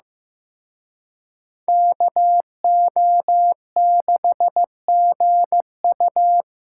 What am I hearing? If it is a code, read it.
Morse code, heard as a keyed tone, about 18 wpm, 700 Hz. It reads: KO6GU
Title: Speed Racing - USA 2x2 Call Signs 15wpm